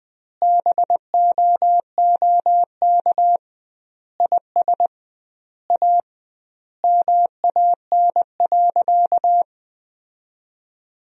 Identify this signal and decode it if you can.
Morse code, heard as a keyed tone, about 20 wpm, 700 Hz. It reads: BOOK IS A MAN.
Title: 2x Word-Spacing - Sentences from Top 500 Words - 20wpm